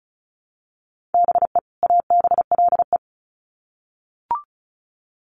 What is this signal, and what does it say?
Morse code, heard as a keyed tone, about 34 wpm, 700 Hz. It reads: BE ABLE
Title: Sets of 2 Words 35wpm